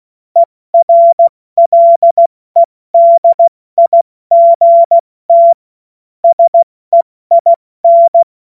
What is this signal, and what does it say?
Morse code, heard as a keyed tone, about 16 wpm, 680 Hz. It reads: ERLEDIGT SEIN
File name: Morsekurs / Witze / Witz 30. August 2022 70 F80